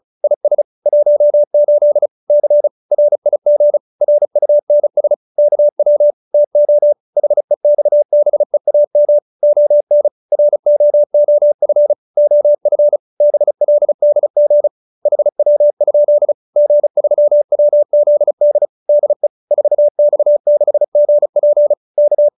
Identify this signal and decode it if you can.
Morse code, heard as a keyed tone, about 35 wpm, 590 Hz. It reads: IS 18 C RIG RUNS KW TO HEXBEAM ON ROOF OF BLDG HW? G3WZD DE 4X6GP K
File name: QSO310 35